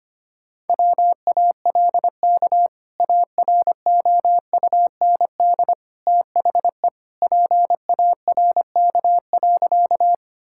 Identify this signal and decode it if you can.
Morse code, heard as a keyed tone, about 25 wpm, 700 Hz. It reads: WALK AROUND THE PARK.
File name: practice / 25wpm / Sentences with 2-Word Phrases - 25wpm